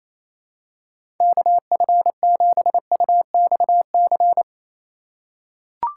Morse code, heard as a keyed tone, about 28 wpm, 700 Hz. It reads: KF7UXC
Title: K1USN SST QSO Practice - 28wpm